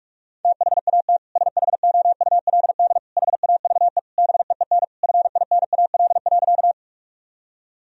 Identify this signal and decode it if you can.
Morse code, heard as a keyed tone, about 47 wpm, 700 Hz. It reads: THAT SHOULD HAVE BEEN FINAL.
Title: Sentences from Top 1000 Words - Review Part 2 - 45wpm